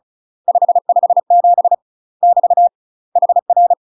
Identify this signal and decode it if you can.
Morse code, heard as a keyed tone, about 34 wpm, 700 Hz. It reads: 557 <BT> HR
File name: UK MASTER QSOs TRAINING / 35WPM / UK MASTER QSOs 0039